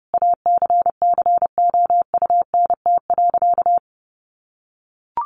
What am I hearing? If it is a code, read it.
Morse code, heard as a keyed tone, about 30 wpm, 700 Hz. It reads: ACCOUNT.
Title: Sentences from Top 1300 Words 30wpm